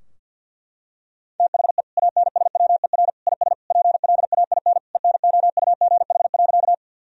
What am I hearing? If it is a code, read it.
Morse code, heard as a keyed tone, about 50 wpm, 700 Hz. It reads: THE ANSWER IS PLAIN ENOUGH.
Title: Sentences from Top 1000 Words - Review Part 2 - 50wpm